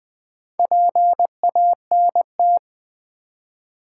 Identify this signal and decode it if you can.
Morse code, heard as a keyed tone, about 20 wpm, 700 Hz. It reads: PANT
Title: Build A Word - 1 Letter At A Time - 5-Letter Words - 20wpm